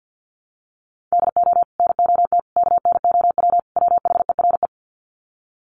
Morse code, heard as a keyed tone, about 50 wpm, 700 Hz. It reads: DO NOT KNOW WHERE
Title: Sets of 4 Words 50wpm